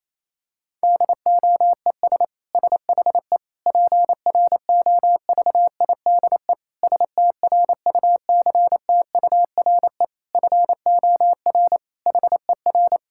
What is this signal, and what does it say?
Morse code, heard as a keyed tone, about 28 wpm, 700 Hz. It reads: DOES SHE PROVIDE STRUCTURE FOR HER
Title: Sentences from Top 1000 Words - Review Part 6 - 28wpm